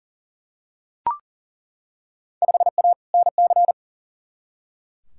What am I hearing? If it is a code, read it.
Morse code, heard as a keyed tone, about 40 wpm, 700 Hz. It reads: 5A NC
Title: Field Day Exchanges - 40wpm